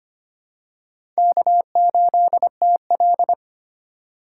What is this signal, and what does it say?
Morse code, heard as a keyed tone, about 25 wpm, 700 Hz. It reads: K8TL
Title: USA Call Signs - 3x Character Spacing - 25wpm